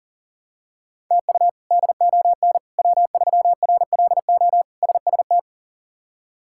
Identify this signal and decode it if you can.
Morse code, heard as a keyed tone, about 40 wpm, 690 Hz. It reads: TU DON W3RLO SST
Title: K1USN SST QSO Practice - 40wpm